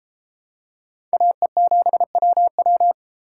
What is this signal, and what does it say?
Morse code, heard as a keyed tone, about 32 wpm, 700 Hz. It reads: AE7WW